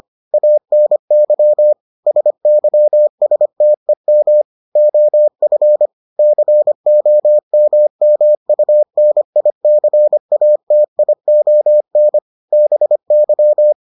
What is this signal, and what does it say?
Morse code, heard as a keyed tone, about 25 wpm, 600 Hz. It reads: ANY SYSTEM OF COMMUNICATION BY